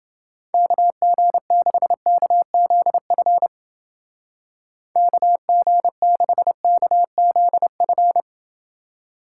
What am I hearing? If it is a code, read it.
Morse code, heard as a keyed tone, about 29 wpm, 700 Hz. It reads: KG6KZF KG6KZF